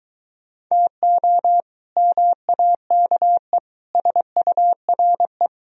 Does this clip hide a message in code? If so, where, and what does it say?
Yes: Morse code from 0.7 to 5.5 s, heard as a keyed tone, about 23 wpm, 700 Hz. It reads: TO MAKE SURE